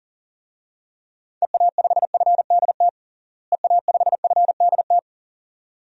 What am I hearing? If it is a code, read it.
Morse code, heard as a keyed tone, about 37 wpm, 700 Hz. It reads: EA5FDT EA5FDT
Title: DXCC-100 Call Signs - 6 Characters - Repeated 2x - 40wpm